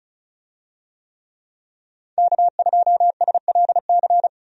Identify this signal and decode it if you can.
Morse code, heard as a keyed tone, about 34 wpm, 700 Hz. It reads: K2SLC
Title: USA Call Signs - 5 Characters - 35wpm